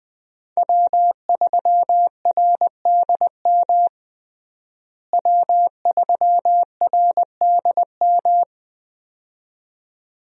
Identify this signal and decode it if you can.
Morse code, heard as a keyed tone, about 20 wpm, 700 Hz. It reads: W3RDM W3RDM
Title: USA Call Signs - 5 Characters - Repeated 2x - 20wpm